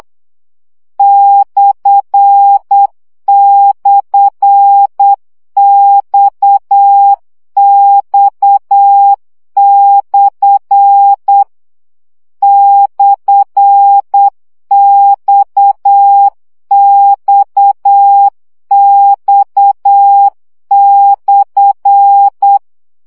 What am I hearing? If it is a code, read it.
Morse code, heard as a keyed tone, about 8 wpm, 800 Hz. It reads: //XX/ /XXX/
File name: CW gyakorlas x per 7 baud